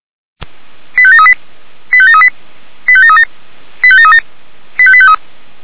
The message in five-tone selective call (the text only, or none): 85738 85738 85737 85737 85853